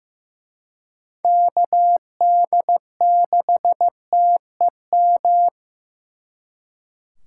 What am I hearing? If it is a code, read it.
Morse code, heard as a keyed tone, about 18 wpm, 700 Hz. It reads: KD6TEM
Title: Speed Racing - USA 2x3 Call Signs 15wpm